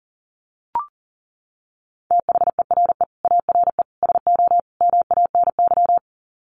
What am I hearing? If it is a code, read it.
Morse code, heard as a keyed tone, about 40 wpm, 700 Hz. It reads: THERE ARE SO MANY
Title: Sets of 4 Words 40wpm